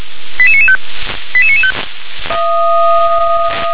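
Five-tone selective call (ZVEI), heard as quick runs of five tones, tones of about 70 ms. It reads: 88995 88995
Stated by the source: ZVEI five-tone selective call 88993 received by SDR.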